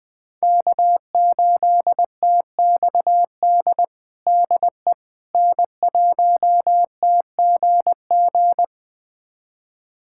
Morse code, heard as a keyed tone, about 20 wpm, 700 Hz. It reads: K8TXD DE N1TGG
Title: USA Callsign DE Callsign - 5 Characters - 20wpm